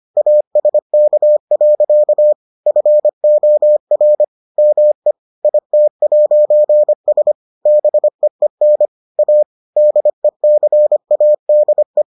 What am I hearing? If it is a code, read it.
Morse code, heard as a keyed tone, about 25 wpm, 600 Hz. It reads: ASK. FOR ME IT'S BEEN A DECADE